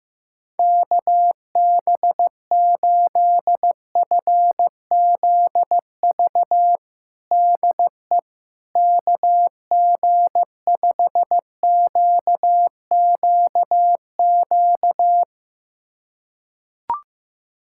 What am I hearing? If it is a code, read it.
Morse code, heard as a keyed tone, about 15 wpm, 700 Hz. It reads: KB8FZV DE KG5QQQ
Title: USA Callsign DE Callsign - 6 Characters - 15wpm